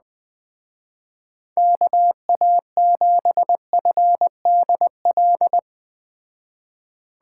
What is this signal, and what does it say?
Morse code, heard as a keyed tone, about 20 wpm, 700 Hz. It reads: KA7FDL
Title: Speed Racing - USA Call Signs 17wpm